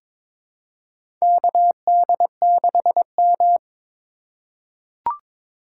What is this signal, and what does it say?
Morse code, heard as a keyed tone, about 22 wpm, 700 Hz. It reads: KD6M
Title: USA Call Signs - 4 Characters - 3x Character Spacing - 22wpm